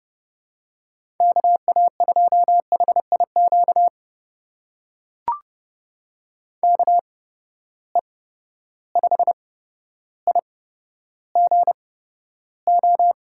Morse code, heard as a keyed tone, about 32 wpm, spaced as for 13 wpm, 700 Hz. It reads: KA2HIQ KE5IGO
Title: USA Call Signs - 6 Characters - 8x Character Spacing - 30wpm